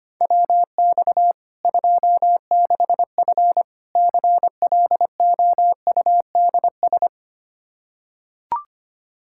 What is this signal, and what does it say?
Morse code, heard as a keyed tone, about 25 wpm, 700 Hz. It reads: WX 26F CLOUDS
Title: Practice QSO - Exchange 2 - Info - 25wpm